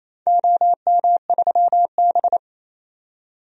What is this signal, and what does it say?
Morse code, heard as a keyed tone, about 28 wpm, 700 Hz. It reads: OM3B
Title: DXCC-100 Call Signs - 4 Characters - 28wpm